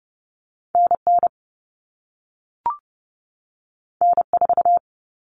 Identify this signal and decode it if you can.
Morse code, heard as a keyed tone, about 30 wpm, 700 Hz. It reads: NN N4
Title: All Cut Numbers - 2-Digit 30wpm